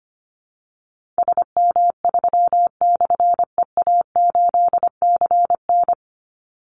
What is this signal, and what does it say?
Morse code, heard as a keyed tone, about 25 wpm, 700 Hz. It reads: SM3/EA8CN
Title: Call Signs Repeated 3x - 25wpm